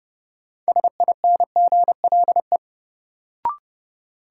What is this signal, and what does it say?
Morse code, heard as a keyed tone, about 30 wpm, 700 Hz. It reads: SINGLE
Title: Sentences from Top 700 Words - New Word Drill - 30wpm